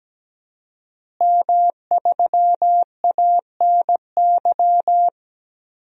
Morse code, heard as a keyed tone, about 17 wpm, 700 Hz. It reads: M3ANY
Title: DXCC-100 Call Signs - 5 Characters - Repeated 3x - 17wpm